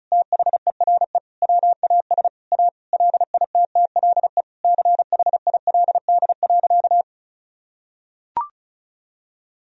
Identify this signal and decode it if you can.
Morse code, heard as a keyed tone, about 36 wpm, 700 Hz. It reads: THERE WAS A LITTLE CHILD.
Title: Sentences from Top 1000 Words - Review Part 1 - 35wpm